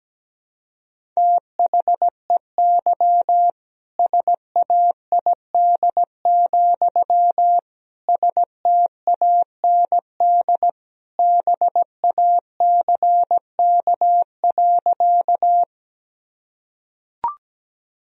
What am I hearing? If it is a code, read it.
Morse code, heard as a keyed tone, about 17 wpm, 700 Hz. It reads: THEY SAID, STAND BACK.
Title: Sentences from Top 1000 Words - Review Part 1 - 17wpm